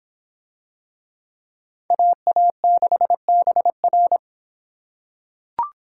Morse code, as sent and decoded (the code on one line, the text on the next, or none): .- .- -.... -... .-.
AA6BR